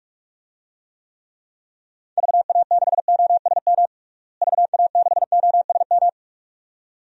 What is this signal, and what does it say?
Morse code, heard as a keyed tone, about 41 wpm, 700 Hz. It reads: VA6OSM VA6OSM
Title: DXCC-100 Call Signs - 6 Characters - Repeated 2x - 45wpm